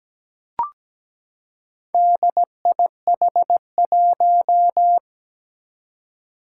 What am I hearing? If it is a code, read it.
Morse code, heard as a keyed tone, about 17 wpm, 700 Hz. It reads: DIH1
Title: Four Challenging Letter-Number Combinations 17wpm